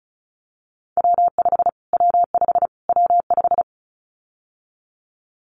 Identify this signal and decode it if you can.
Morse code, heard as a keyed tone, about 34 wpm, 700 Hz. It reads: W5 W5 W5
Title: Callsign Prefixes for Top 25 DXCC - Repeated 3x - 35wpm